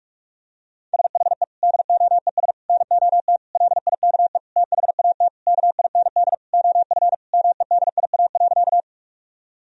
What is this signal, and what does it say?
Morse code, heard as a keyed tone, about 43 wpm, 700 Hz. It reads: SHE DOES NOT LIKE THAT KIND OF MEDIA.
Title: Sentences from Top 1000 Words - Review Part 4 - 45wpm